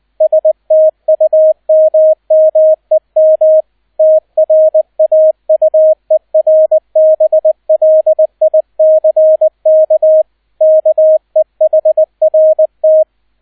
Morse code, heard as a keyed tone, about 20 wpm, 630 Hz, with radio noise. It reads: STUMMEM TRAUERBLICK KEHRT
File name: Erster Streich 20wpm